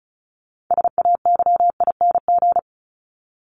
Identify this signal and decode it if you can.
Morse code, heard as a keyed tone, about 34 wpm, 700 Hz. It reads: SAYING
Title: ING Suffix - Top 100 Words with Base Words 35wpm